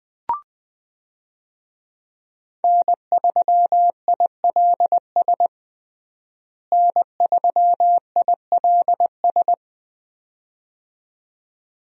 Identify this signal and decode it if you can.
Morse code, heard as a keyed tone, about 20 wpm, 700 Hz. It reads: N3ILS N3ILS
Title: USA Call Signs - 5 Characters - Repeated 2x - 20wpm